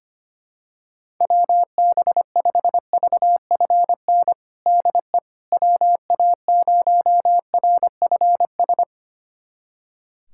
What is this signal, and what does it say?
Morse code, heard as a keyed tone, about 25 wpm, 700 Hz. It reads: WB5VFN DE WA0RFS
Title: USA Callsign DE Callsign - 6 Characters - 25wpm